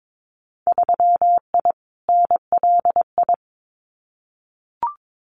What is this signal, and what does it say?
Morse code, heard as a keyed tone, about 22 wpm, 700 Hz. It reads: 3I NLI
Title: Winter Field Day Exchanges 22wpm